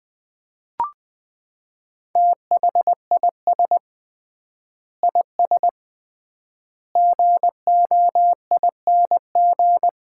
Morse code, heard as a keyed tone, about 21 wpm, 700 Hz. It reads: THIS IS GOING
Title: 3x Word-Spacing - Sets of 4 Words - 20wpm